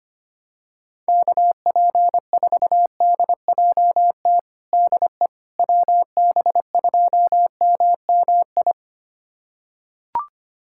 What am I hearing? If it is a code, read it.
Morse code, heard as a keyed tone, about 25 wpm, 700 Hz. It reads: KP4DJT DE WB2MMI
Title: USA Callsign DE Callsign - 6 Characters - 25wpm